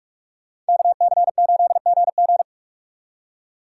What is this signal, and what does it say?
Morse code, heard as a keyed tone, about 63 wpm, spaced as for 54 wpm, 700 Hz. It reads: KC8GG
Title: Speed Racing - USA 2x2 Call Signs 45wpm